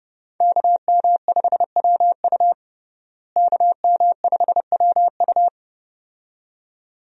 Morse code, heard as a keyed tone, about 32 wpm, 700 Hz. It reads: KM5WU KM5WU
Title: USA Call Signs - 5 Characters - Repeated 2x - 30wpm